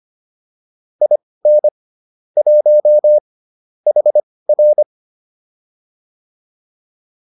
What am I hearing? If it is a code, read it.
Morse code, heard as a keyed tone, about 25 wpm, spaced as for 18 wpm, 600 Hz. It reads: IN 1 HR
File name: PR103 18